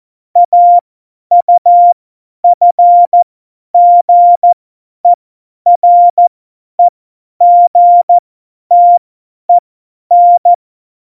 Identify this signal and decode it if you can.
Morse code, heard as a keyed tone, about 14 wpm, spaced as for 10 wpm, 700 Hz. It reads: AUFGEREGTEN
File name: Morsekurs / Witze / Witz 17.Dezember 2019, 50 F70